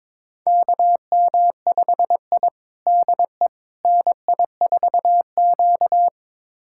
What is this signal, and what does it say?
Morse code, heard as a keyed tone, about 22 wpm, 700 Hz. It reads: KM5I DE NI4Q
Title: USA Callsign DE Callsign - 4 Characters - 22wpm